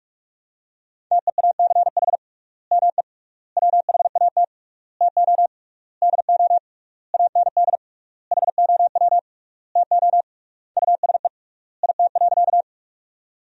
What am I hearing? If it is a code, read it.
Morse code, heard as a keyed tone, about 41 wpm, 700 Hz. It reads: TEACH ME WHAT TO DO AND HOW TO USE IT.
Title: 3x Word-Spacing - Sentences from Top 500 Words - 45wpm